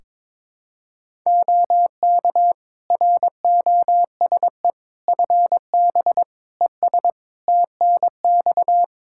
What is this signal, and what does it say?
Morse code, heard as a keyed tone, about 22 wpm, 700 Hz. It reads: OK ROSE FB ES TNX
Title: Practice QSO - Exchange 2 - Info - 22wpm